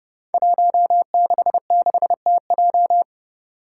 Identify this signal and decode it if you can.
Morse code, heard as a keyed tone, about 30 wpm, 700 Hz. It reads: 166TJ
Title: Five Challenging Letter-Number Combinations 30wpm